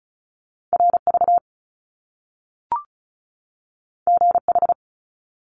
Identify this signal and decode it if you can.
Morse code, heard as a keyed tone, about 36 wpm, 700 Hz. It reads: RV GH
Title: Top Two Letter Combinations in Words 35wpm